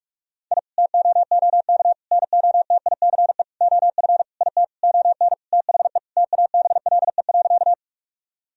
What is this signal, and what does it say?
Morse code, heard as a keyed tone, about 46 wpm, 700 Hz. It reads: I TOOK NOTICE OF IT ON THE TABLE.
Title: Sentences from Top 1000 Words - Review Part 2 - 45wpm